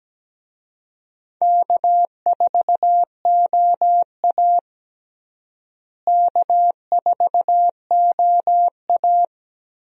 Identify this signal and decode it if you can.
Morse code, heard as a keyed tone, about 17 wpm, 700 Hz. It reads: K4OA K4OA
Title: USA Call Signs - 4 Characters - Repeated 2x - 17wpm